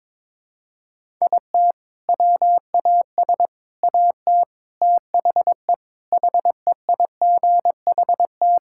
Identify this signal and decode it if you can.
Morse code, heard as a keyed tone, about 22 wpm, 700 Hz. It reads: IT WAS AT THE HEIGHT
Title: Sentences from Top 1600 Words 22wpm